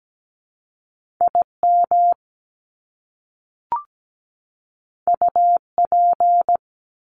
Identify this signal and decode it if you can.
Morse code, heard as a keyed tone, about 17 wpm, 700 Hz. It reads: IM UP
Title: Top Two Letter Combinations in Words 17wpm